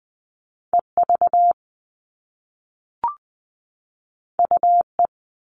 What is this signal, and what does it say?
Morse code, heard as a keyed tone, about 20 wpm, 700 Hz. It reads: EV UE
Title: All Cut Numbers - 2-Digit 20wpm